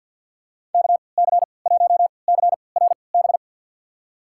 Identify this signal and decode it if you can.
Morse code, heard as a keyed tone, about 53 wpm, spaced as for 29 wpm, 700 Hz. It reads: KC1CRB
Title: USA Call Signs - 6 Characters - 3x Character Spacing - 50wpm